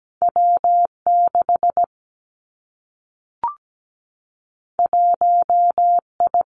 Morse code, heard as a keyed tone, about 17 wpm, 700 Hz. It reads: W6 1I
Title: Two Challenging Letter-Number Combinations 17wpm